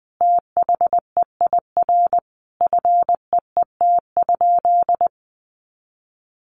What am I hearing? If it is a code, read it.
Morse code, heard as a keyed tone, about 20 wpm, 700 Hz. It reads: THEIR FEET?
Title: Sentences from Top 1500 Words 20wpm